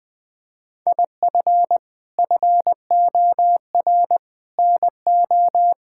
Morse code, heard as a keyed tone, about 20 wpm, 700 Hz. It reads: IF FOR NO
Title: Sentences with 6-Word Phrases - 20wpm